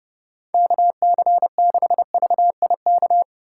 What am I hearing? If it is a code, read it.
Morse code, heard as a keyed tone, about 30 wpm, 700 Hz. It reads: KC6VIK